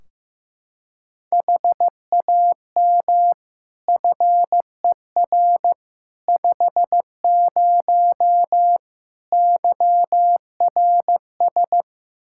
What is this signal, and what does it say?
Morse code, heard as a keyed tone, about 15 wpm, 700 Hz. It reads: HAM FER 50 YRS
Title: Practice QSO - Exchange 3 - More Info - 15wpm